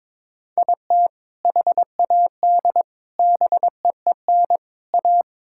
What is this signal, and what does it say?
Morse code, heard as a keyed tone, about 22 wpm, 700 Hz. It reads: IT HAD BEEN A
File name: practice / 22wpm / Sentences with 2-Word Phrases - 22wpm